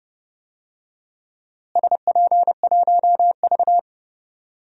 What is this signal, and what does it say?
Morse code, heard as a keyed tone, about 32 wpm, 700 Hz. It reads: SP1V
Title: DXCC-100 Call Signs - 4 Characters - Repeated 3x - 30wpm